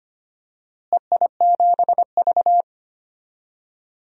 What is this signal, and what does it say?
Morse code, heard as a keyed tone, about 25 wpm, 700 Hz. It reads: EI7V
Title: DXCC-100 Call Signs - 4 Characters - Repeated 2x - 25wpm